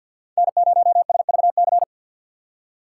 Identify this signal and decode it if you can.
Morse code, heard as a keyed tone, about 61 wpm, 700 Hz. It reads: N0SVC
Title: Speed Racing - USA 1x3 Call Signs 50wpm